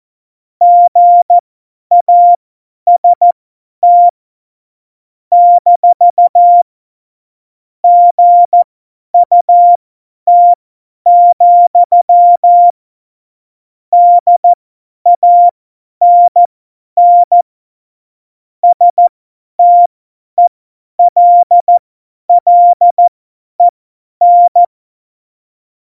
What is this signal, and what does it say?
Morse code, heard as a keyed tone, about 14 wpm, spaced as for 10 wpm, 700 Hz. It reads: GAST - GUT, DANN STELLEN
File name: Morsekurs / Witze / Witz 23. Juli 2019, 50 F70